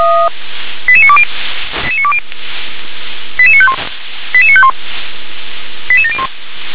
Five-tone selective call (ZVEI), heard as quick runs of five tones, tones of about 70 ms. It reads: 88920 88951 88951 88701
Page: /357/SDR/ZVEI/